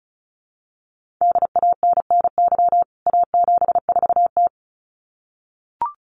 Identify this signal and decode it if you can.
Morse code, heard as a keyed tone, about 34 wpm, 700 Hz. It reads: DANNY A74T
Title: CWT Exchange 35wpm